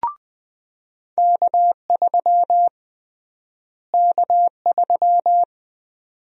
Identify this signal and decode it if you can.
Morse code, heard as a keyed tone, about 20 wpm, 700 Hz. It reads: K3 K3
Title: USA Call Sign Prefixes - Repeated 2x - 20wpm